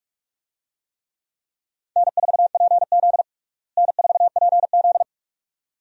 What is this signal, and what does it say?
Morse code, heard as a keyed tone, about 46 wpm, 700 Hz. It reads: N4PZ N4PZ
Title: USA Call Signs - 4 Characters - Repeated 2x - 45wpm